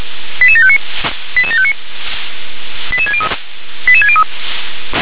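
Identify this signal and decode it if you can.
Five-tone selective call (ZVEI), heard as quick runs of five tones, tones of about 70 ms. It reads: 88759 88759 88603 88603
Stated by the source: ZVEI five-tone selective call 88758 received on an SDR.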